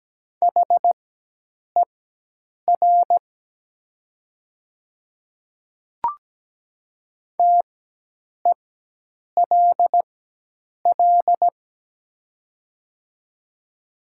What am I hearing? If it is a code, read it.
Morse code, heard as a keyed tone, about 17 wpm, spaced as for 8 wpm, 700 Hz. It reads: HER TELL
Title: 4x Character Spacing - Top 500 Words - 17wpm